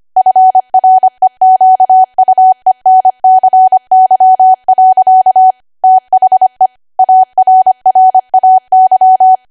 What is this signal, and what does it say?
Morse code, heard as a keyed tone, about 25 wpm, 750 Hz. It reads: FREQUENCY. THE ARRAY
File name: Archive / 25 WPM / 25WPM